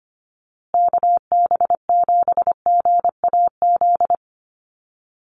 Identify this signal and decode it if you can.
Morse code, heard as a keyed tone, about 25 wpm, 700 Hz. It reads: KB7GAZ
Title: USA Call Signs Repeated 2x 25wpm